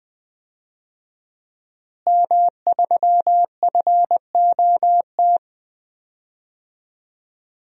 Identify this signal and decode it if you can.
Morse code, heard as a keyed tone, about 20 wpm, 700 Hz. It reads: M3FOT
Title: DXCC-100 Call Signs - 5 Characters - Repeated 3x - 20wpm